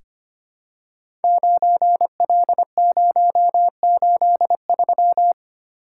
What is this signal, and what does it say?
Morse code, heard as a keyed tone, about 25 wpm, 700 Hz. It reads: 9L083
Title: Code Groups - 5 Letter:Numbers - 25wpm